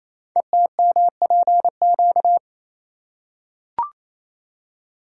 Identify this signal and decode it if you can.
Morse code, heard as a keyed tone, about 27 wpm, 700 Hz. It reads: ETMPQ
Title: Code Groups - 5 Letters - 28wpm